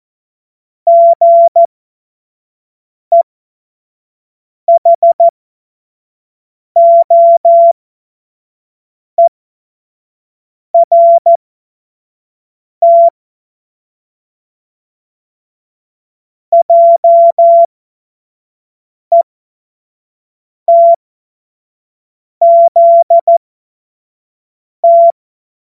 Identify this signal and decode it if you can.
Morse code, heard as a keyed tone, about 14 wpm, spaced as for 5 wpm, 680 Hz. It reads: GEHOERT JETZT
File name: Morsekurs / Witze / Witz 27. Juni 2023 25 F70